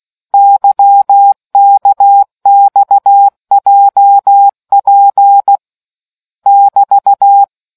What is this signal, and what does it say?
Morse code, heard as a keyed tone, about 16 wpm, 800 Hz. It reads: YKXJP <BT>